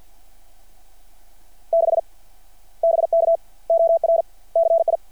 Morse code, heard as a keyed tone, about 53 wpm, 660 Hz. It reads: 6 BK OR CI
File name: archived / CRSnet